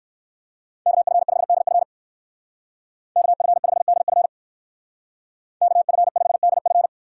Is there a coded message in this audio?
Yes: Morse code from 0.9 to 6.9 s, heard as a keyed tone, about 59 wpm, 700 Hz. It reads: KF5DF KF5DF KF5DF